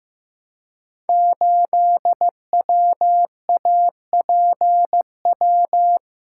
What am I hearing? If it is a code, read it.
Morse code, heard as a keyed tone, about 15 wpm, 700 Hz. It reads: 8WAPW
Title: Code Groups - 5 Letter:Numbers - 15wpm